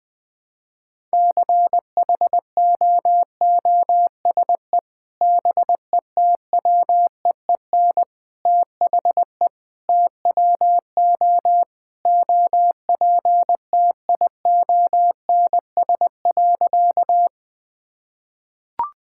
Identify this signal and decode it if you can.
Morse code, heard as a keyed tone, about 20 wpm, 700 Hz. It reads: CHOOSE BETWEEN THE TWO OPTIONS.
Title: Sentences with 3-Word Phrases - 20wpm